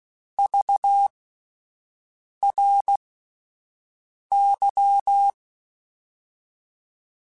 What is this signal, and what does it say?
Morse code, heard as a keyed tone, about 16 wpm, spaced as for 5 wpm, 800 Hz. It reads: VRY
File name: QSOs Spaced